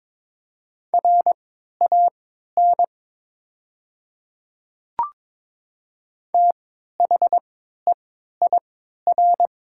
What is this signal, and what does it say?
Morse code, heard as a keyed tone, about 22 wpm, spaced as for 12 wpm, 700 Hz. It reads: RAN THEIR
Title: 3x Character Spacing - Top 500 Words - 22wpm